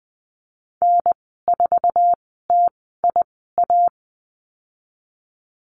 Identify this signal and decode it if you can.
Morse code, heard as a keyed tone, about 20 wpm, spaced as for 14 wpm, 700 Hz. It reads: N4TIA
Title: 069 Lesson 12g - Letter d - Call Signs 20wpm